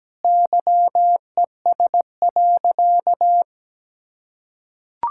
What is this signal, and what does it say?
Morse code, heard as a keyed tone, about 17 wpm, 700 Hz. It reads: YES.
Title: Sentences from Top 1000 Words - Review Part 6 - 17wpm